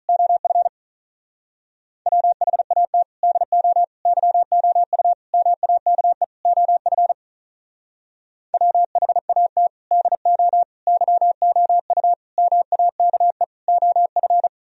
Morse code, heard as a keyed tone, about 44 wpm, 700 Hz. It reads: OF WHAT DO YOU MAKE OF WHAT DO YOU MAKE OF
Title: Speed-Racing - Sets of 5 Words 35wpm